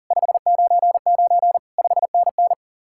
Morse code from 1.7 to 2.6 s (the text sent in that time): 5NN